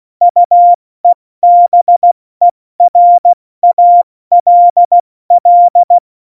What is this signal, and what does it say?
Morse code, heard as a keyed tone, about 16 wpm, spaced as for 14 wpm, 700 Hz. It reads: UEBERALL
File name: Morsekurs / Witze / Witz 06. Februar 2018, 70 F80